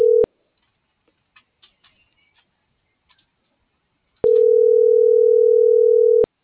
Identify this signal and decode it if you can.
Ringback tone (North America), 440 + 480 Hz.